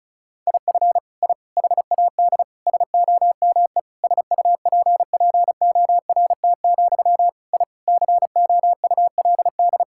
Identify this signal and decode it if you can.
Morse code, heard as a keyed tone, about 34 wpm, 700 Hz. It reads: IF I HAD SOME SUPPORT, I COULD
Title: Sentences from Top 1000 Words - Review Part 5 - 35wpm